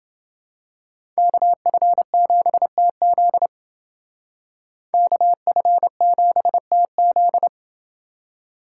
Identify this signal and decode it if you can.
Morse code, heard as a keyed tone, about 29 wpm, 700 Hz. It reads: KF7TZ KF7TZ